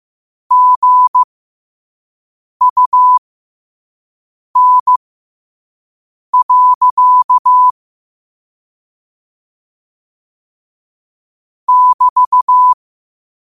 Morse code, heard as a keyed tone, about 15 wpm, spaced as for 5 wpm, 1000 Hz. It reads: GUN. <BT>
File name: qotdfiles / 5wpm